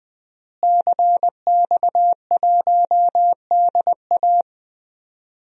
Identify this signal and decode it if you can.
Morse code, heard as a keyed tone, about 20 wpm, 700 Hz. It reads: CX1DA
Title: DXCC-100 Call Signs - 5 Characters - Repeated 3x - 20wpm